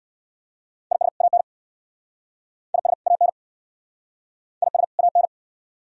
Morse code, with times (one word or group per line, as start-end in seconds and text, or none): 0.9-1.1 IS
1.2-1.4 AN
2.7-2.9 IS
3.1-3.3 AN
4.6-4.8 IS
5.0-5.3 AN